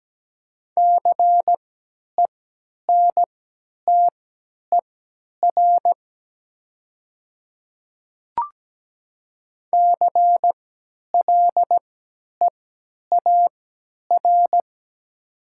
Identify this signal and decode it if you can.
Morse code, heard as a keyed tone, about 17 wpm, spaced as for 10 wpm, 690 Hz. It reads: CENTER CLEAR
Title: 3x Character Spacing - Top 500 Words - 17wpm